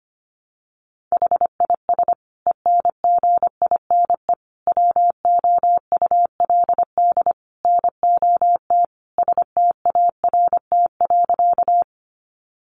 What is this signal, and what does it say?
Morse code, heard as a keyed tone, about 25 wpm, 700 Hz. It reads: HIS ENGINE WOULD NOT START.
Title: Sentences from Top 1500 Words 25wpm